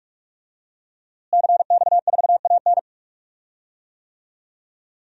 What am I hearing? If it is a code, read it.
Morse code, heard as a keyed tone, about 46 wpm, 700 Hz. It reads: CX4AN
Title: DXCC-100 Call Signs - 5 Characters - Repeated 2x - 45wpm